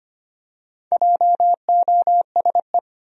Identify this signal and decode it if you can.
Morse code, heard as a keyed tone, about 25 wpm, 700 Hz. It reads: JOSE
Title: USA Common Names - Max 4 Letters - 25wpm